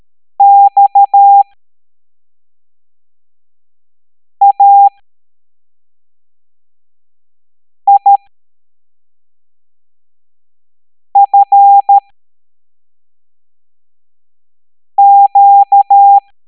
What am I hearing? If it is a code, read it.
Morse code, heard as a keyed tone, about 13 wpm, spaced as for 3 wpm, 800 Hz. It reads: XAIFQ
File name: hamdata / L07-EX03-HQP